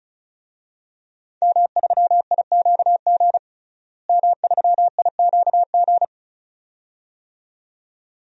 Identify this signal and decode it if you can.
Morse code, heard as a keyed tone, about 34 wpm, 700 Hz. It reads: M3IQG M3IQG
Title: DXCC-100 Call Signs - 5 Characters - Repeated 2x - 35wpm